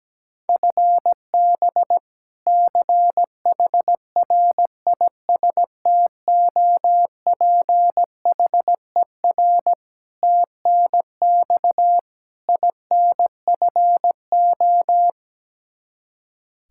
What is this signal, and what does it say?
Morse code, heard as a keyed tone, about 17 wpm, 700 Hz. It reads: FB CHRISTOPHER TNX INFO
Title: K1USN SST QSO Practice 17wpm